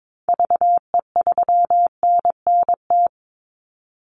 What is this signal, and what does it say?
Morse code, heard as a keyed tone, about 22 wpm, 700 Hz. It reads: VE3NNT
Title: Call Signs Repeated 3x - 22wpm